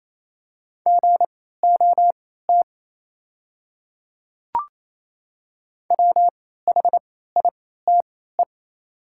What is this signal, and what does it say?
Morse code, heard as a keyed tone, about 28 wpm, spaced as for 16 wpm, 700 Hz. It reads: GOT WHITE
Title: 3x Character Spacing - Top 300 Words - 28wpm